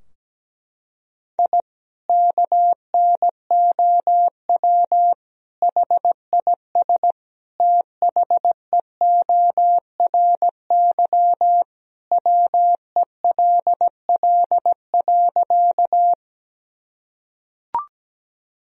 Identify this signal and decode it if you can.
Morse code, heard as a keyed tone, about 17 wpm, 700 Hz. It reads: I KNOW HIS THEORY WELL.
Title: Sentences from Top 1000 Words - Review Part 3 - 17wpm